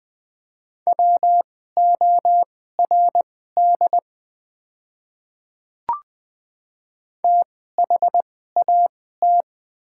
Morse code, heard as a keyed tone, about 20 wpm, spaced as for 14 wpm, 700 Hz. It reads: WORD THAT
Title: 2x Character Spacing - Top 100 Words - 20wpm